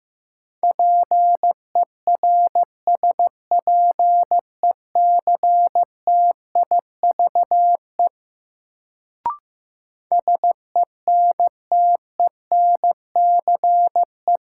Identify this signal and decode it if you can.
Morse code, heard as a keyed tone, about 15 wpm, 700 Hz. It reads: PERSPECTIVE SENTENCE
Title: Sentences from Top 1200 Words - New Word Drill Alt - 15wpm